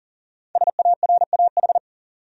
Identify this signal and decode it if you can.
Morse code, heard as a keyed tone, about 38 wpm, 700 Hz. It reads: SARAH
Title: USA Common Names - Max 5 Letters - 40wpm